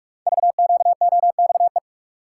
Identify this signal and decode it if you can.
Morse code, heard as a keyed tone, about 44 wpm, 700 Hz. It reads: VQOXE
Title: Code Groups - 5 Letters - 45wpm